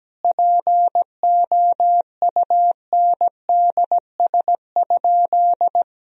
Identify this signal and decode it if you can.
Morse code, heard as a keyed tone, about 17 wpm, 700 Hz. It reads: POUNDS?
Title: Sentences from Top 1000 Words - Review Part 2 - 17wpm